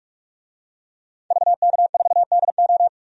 Morse code, heard as a keyed tone, about 42 wpm, 700 Hz. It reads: VK4DO
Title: DXCC-100 Call Signs - 5 Characters - 45wpm